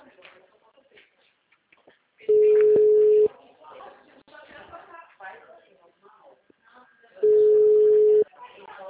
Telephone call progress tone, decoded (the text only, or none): ringback tone